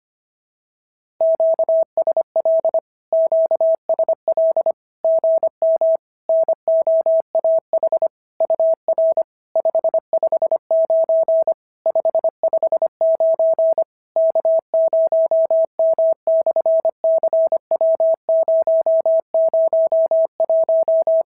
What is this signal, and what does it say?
Morse code, heard as a keyed tone, about 25 wpm, 650 Hz. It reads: QSL QSL GM NOAH UR 559 559 K0M/CW001
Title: SOTA QSO Practice 25wpm